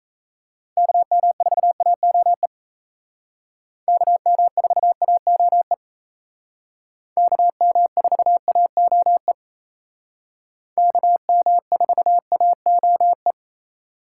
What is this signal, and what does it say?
Morse code, heard as a keyed tone, about 35 wpm, 700 Hz. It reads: KM4AOE KM4AOE KM4AOE KM4AOE